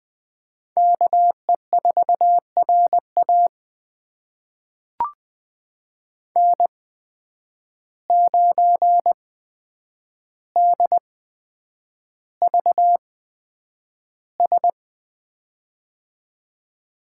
Morse code, heard as a keyed tone, about 20 wpm, spaced as for 9 wpm, 700 Hz. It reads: KE4RA N9DVS
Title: USA Call Signs - 5 Characters - 8x Character Spacing - 20wpm